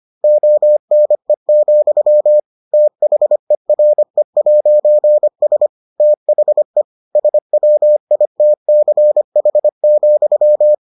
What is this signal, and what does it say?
Morse code, heard as a keyed tone, about 25 wpm, 600 Hz. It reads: ONE, THERE'S THE SWITCH,